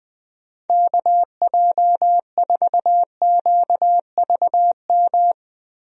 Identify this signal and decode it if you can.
Morse code, heard as a keyed tone, about 20 wpm, 700 Hz. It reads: KJ4QVM
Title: USA Call Signs - 6 Characters - Repeated 2x - 20wpm